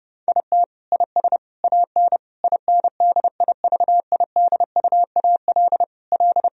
Morse code, heard as a keyed tone, about 30 wpm, 700 Hz. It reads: IT IS AN INDIVIDUAL L
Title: Sentences from Top 1000 Words - Review Part 4 - 30wpm